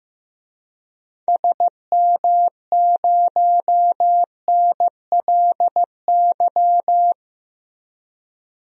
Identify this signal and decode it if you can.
Morse code, heard as a keyed tone, about 15 wpm, 700 Hz. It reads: SM0NLY
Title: DXCC-100 Call Signs - 6 Characters - 15wpm